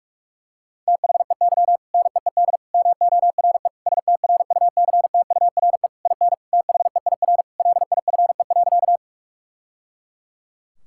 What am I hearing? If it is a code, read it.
Morse code, heard as a keyed tone, about 46 wpm, 700 Hz. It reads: THEY NEED MORE STRUCTURE IN THEIR LIFE.
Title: Sentences from Top 1000 Words - Review Part 6 - 45wpm